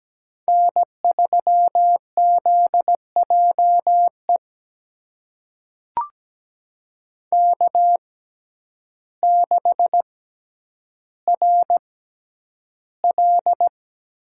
Morse code, heard as a keyed tone, about 17 wpm, spaced as for 9 wpm, 700 Hz. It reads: N3ZJE K6RL
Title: USA Call Signs - 5 Characters - 6x Character Spacing - 17wpm